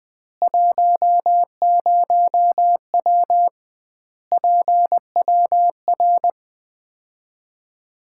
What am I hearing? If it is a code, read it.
Morse code, heard as a keyed tone, about 20 wpm, 700 Hz. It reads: 10W PWR
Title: Common QSO Phrases - 2x Word-Spacing - 20wpm